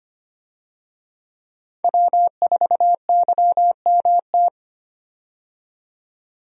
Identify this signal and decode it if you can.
Morse code, heard as a keyed tone, about 25 wpm, 700 Hz. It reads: W4YMT
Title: USA Call Signs - 5 Characters - 25wpm